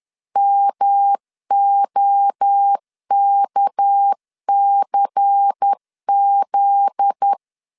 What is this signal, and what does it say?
Morse code, heard as a keyed tone, about 10 wpm, 800 Hz. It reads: MOKCZ
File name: LEZIONI / 8 YJ45 / 8 YJ 10wpm 2651